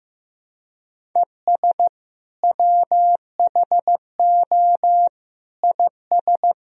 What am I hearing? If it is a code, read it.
Morse code, heard as a keyed tone, about 16 wpm, 700 Hz. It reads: ES WHO IS
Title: Sentences from Top 100 Words and Abbreviations 15wpm